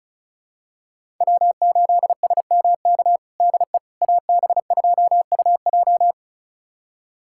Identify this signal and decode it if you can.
Morse code, heard as a keyed tone, about 34 wpm, 700 Hz. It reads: W8SMK DE AB2UJ
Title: USA Callsign DE Callsign - 5 Characters - 35wpm